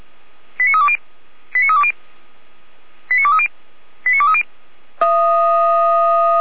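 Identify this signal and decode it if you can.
Five-tone selective call (ZVEI), heard as quick runs of five tones, tones of about 70 ms. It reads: 89320 89320 89230 89230
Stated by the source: ZVEI five-tone call received with an SDR.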